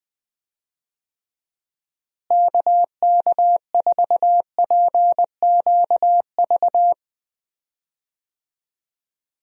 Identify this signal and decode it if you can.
Morse code, heard as a keyed tone, about 20 wpm, 700 Hz. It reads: KK4PQV
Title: USA Call Signs - 6 Characters - Repeated 2x - 20wpm